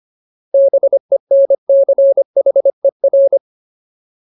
Morse code, heard as a keyed tone, about 25 wpm, 550 Hz. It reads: BENCHER
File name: Ultimate Word List / 25WPM Ultimate